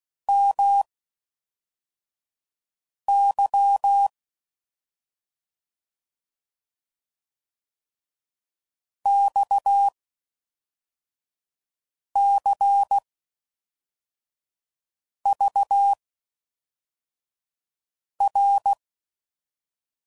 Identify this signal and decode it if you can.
Morse code, heard as a keyed tone, about 16 wpm, spaced as for 4 wpm, 800 Hz. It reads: MY XCVR